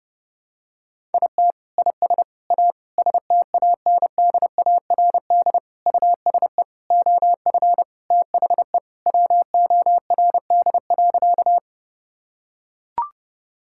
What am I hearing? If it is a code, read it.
Morse code, heard as a keyed tone, about 30 wpm, 700 Hz. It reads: IT IS A STANDARD USE OF THE WORD.
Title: Sentences from Top 1000 Words - Review Part 6 - 30wpm